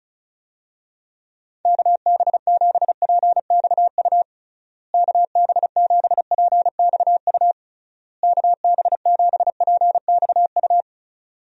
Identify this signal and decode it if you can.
Morse code, heard as a keyed tone, about 34 wpm, 700 Hz. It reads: KB7PXU KB7PXU KB7PXU
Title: USA Call Signs - 6 Characters - Repeated 3x - 35wpm